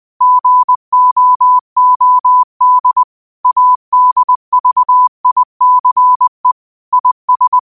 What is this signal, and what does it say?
Morse code, heard as a keyed tone, about 20 wpm, 1000 Hz. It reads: GOOD ADVICE IS